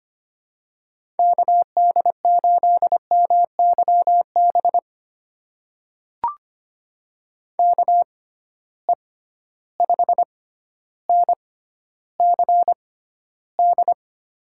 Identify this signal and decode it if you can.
Morse code, heard as a keyed tone, about 25 wpm, spaced as for 13 wpm, 700 Hz. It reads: KD8MYB KE5NCD
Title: USA Call Signs - 6 Characters - 6x Character Spacing - 25wpm